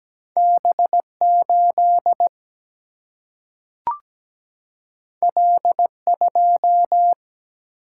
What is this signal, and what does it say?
Morse code, heard as a keyed tone, about 17 wpm, 700 Hz. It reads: B 8 L 2
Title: Single-Letter Single-Number 17wpm